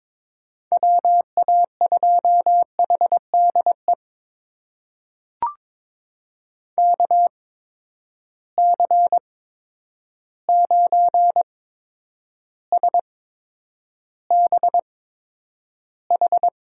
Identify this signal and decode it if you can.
Morse code, heard as a keyed tone, about 22 wpm, spaced as for 9 wpm, 700 Hz. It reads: WA2HDE KC9SBH
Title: USA Call Signs - 6 Characters - 8x Character Spacing - 22wpm